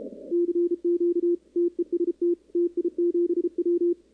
Morse code, heard as a keyed tone, about 28 wpm, 340 Hz, with radio noise. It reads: CQ TEST TI7W
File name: contests / 2019-cw-wpx-kt5j-k5pi- -2330 0-11